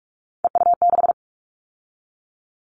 Morse code, heard as a keyed tone, about 46 wpm, 700 Hz. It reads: EU6
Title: Callsign Prefixes for Top 50 DXCC - 45wpm